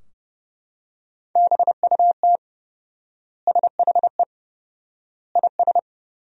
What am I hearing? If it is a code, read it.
Morse code, heard as a keyed tone, about 30 wpm, 700 Hz. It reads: BUT SHE IS
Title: 4x Word-Spacing - Sentences from Top 500 Words - 30wpm